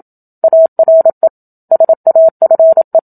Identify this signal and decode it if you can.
Morse code, heard as a keyed tone, about 27 wpm, 650 Hz. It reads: ARE SAFE
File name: Book Two / 01-Chapter 1 - 27 WPM